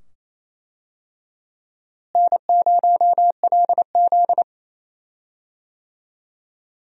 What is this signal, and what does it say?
Morse code, heard as a keyed tone, about 28 wpm, 700 Hz. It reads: N0LZ
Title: USA Call Signs - 4 Characters - Repeated 2x - 28wpm